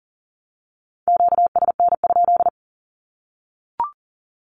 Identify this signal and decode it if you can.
Morse code, heard as a keyed tone, about 40 wpm, 700 Hz. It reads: QSN?
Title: Q-Codes Repeated 3x 40wpm